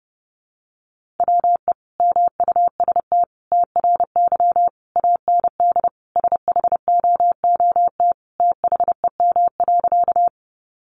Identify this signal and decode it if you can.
Morse code, heard as a keyed tone, about 30 wpm, 700 Hz. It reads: WE MUST TRY AND SHOOT THEM.
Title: Sentences from Top 1200 Words 30wpm